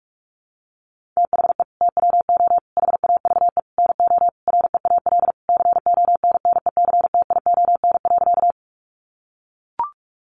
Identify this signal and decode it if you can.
Morse code, heard as a keyed tone, about 44 wpm, 700 Hz. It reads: THE TWO HAVE NO REAL CONNECTION.
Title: Sentences from Top 1300 Words 45wpm